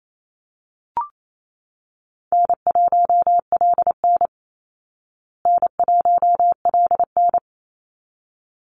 Morse code, heard as a keyed tone, about 28 wpm, 700 Hz. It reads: N1LN N1LN
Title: Call Signs Repeated 3x - 28wpm